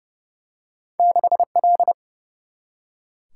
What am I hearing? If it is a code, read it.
Morse code, heard as a keyed tone, about 30 wpm, 700 Hz. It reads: 6L
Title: Single-Letter Single-Number 30wpm